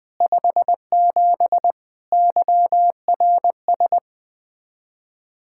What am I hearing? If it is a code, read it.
Morse code, heard as a keyed tone, about 20 wpm, 700 Hz. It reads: 57 YRS
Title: Speed-Racing Common QSO Phrases - 20wpm